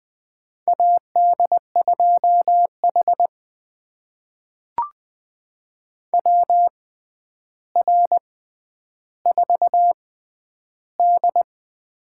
Morse code, heard as a keyed tone, about 20 wpm, spaced as for 10 wpm, 700 Hz. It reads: AD2H WR4D
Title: USA Call Signs - 4 Characters - 6x Character Spacing - 20wpm